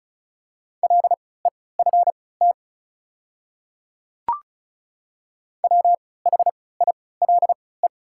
Morse code, heard as a keyed tone, about 34 wpm, spaced as for 20 wpm, 690 Hz. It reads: LEFT WHILE
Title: 3x Character Spacing - Top 300 Words - 35wpm